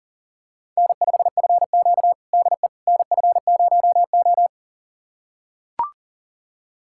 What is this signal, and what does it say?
Morse code, heard as a keyed tone, about 40 wpm, 700 Hz. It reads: N5FQ DE NF0O
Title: USA Callsign DE Callsign - 4 Characters - 40wpm